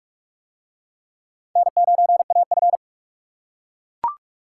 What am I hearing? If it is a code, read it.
Morse code, heard as a keyed tone, about 44 wpm, 700 Hz. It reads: N9AF